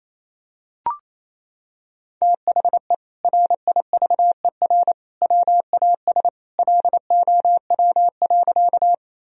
Morse code, heard as a keyed tone, about 28 wpm, 700 Hz. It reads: THE RIVER WAS LOW.
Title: Sentences from Top 1000 Words - Review Part 1 - 28wpm